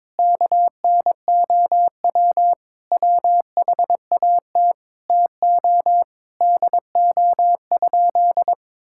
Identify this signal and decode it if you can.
Morse code, heard as a keyed tone, about 22 wpm, 700 Hz. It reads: KNOW WHAT TO DO?
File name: practice / 22wpm / Sentences with 2-Word Phrases - 22wpm